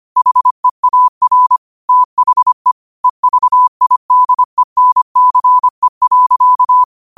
Morse code, heard as a keyed tone, about 25 wpm, 1000 Hz. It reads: HEAR THE EVIDENCE.
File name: qotdfiles / 25wpm